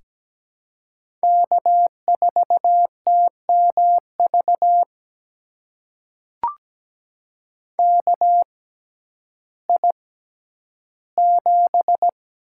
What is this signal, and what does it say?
Morse code, heard as a keyed tone, about 17 wpm, spaced as for 10 wpm, 700 Hz. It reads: K4TMV KI7
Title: USA Call Signs - 6x Character Spacing - 17wpm